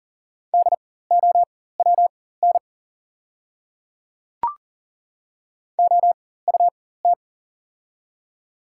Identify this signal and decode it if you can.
Morse code, heard as a keyed tone, about 40 wpm, spaced as for 19 wpm, 700 Hz. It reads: DOWN OUT
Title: 4x Character Spacing - Top 100 Words - 40wpm